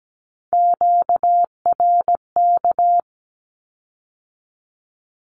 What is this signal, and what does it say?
Morse code, heard as a keyed tone, about 17 wpm, 700 Hz. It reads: QRK
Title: Q-Codes with Spoken Meaning - 17wpm